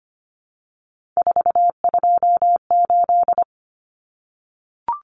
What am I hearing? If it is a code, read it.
Morse code, heard as a keyed tone, about 25 wpm, 700 Hz. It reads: 428
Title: Common Cut Numbers - 3-Digit 25wpm